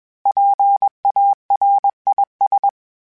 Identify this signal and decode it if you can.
Morse code, heard as a keyed tone, about 21 wpm, 800 Hz. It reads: PARIS
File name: PARIS Calibration / PARIS mp3s / PARIS@20